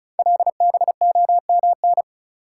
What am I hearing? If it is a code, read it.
Morse code, heard as a keyed tone, about 36 wpm, 690 Hz. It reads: LBOMN
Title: Code Groups - 5 Letters - 35wpm